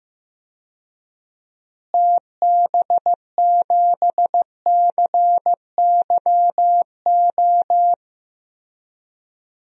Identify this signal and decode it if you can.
Morse code, heard as a keyed tone, about 15 wpm, 700 Hz. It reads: TB7CYO
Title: DXCC-100 Call Signs - 6 Characters - Repeated 3x - 15wpm